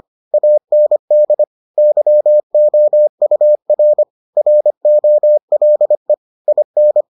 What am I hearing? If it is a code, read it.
Morse code, heard as a keyed tone, about 25 wpm, 600 Hz. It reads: AND YOUR ROLE IN